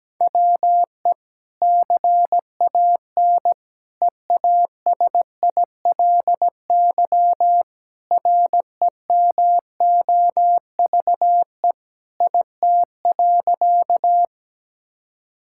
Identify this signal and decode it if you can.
Morse code, heard as a keyed tone, about 17 wpm, 700 Hz. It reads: WE CAN EASILY REMOVE IT.
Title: Sentences from Top 1000 Words - Review Part 5 - 17wpm